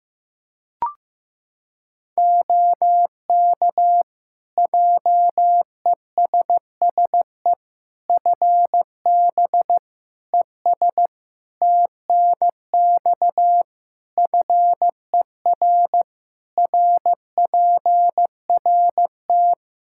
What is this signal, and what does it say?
Morse code, heard as a keyed tone, about 15 wpm, 700 Hz. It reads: OK JESSE FB ES TNX FER RPRT
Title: Practice QSO - Exchange 2 - Info - 15wpm